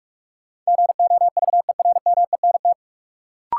Morse code, heard as a keyed tone, about 46 wpm, 700 Hz. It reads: GOVERMENT
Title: Sentences from Top 1300 Words - New Word Drill - 45wpm